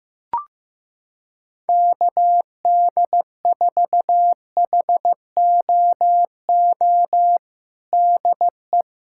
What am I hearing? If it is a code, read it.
Morse code, heard as a keyed tone, about 15 wpm, 700 Hz. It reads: KD4HOO DE
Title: Practice QSO - Exchange 4 - Ending - 15wpm